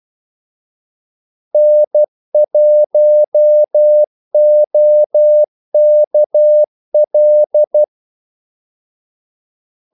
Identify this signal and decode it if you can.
Morse code, heard as a keyed tone, about 12 wpm, 600 Hz. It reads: N1OKL
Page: 2 ekstra ordmellemrum mellem kaldesignalerne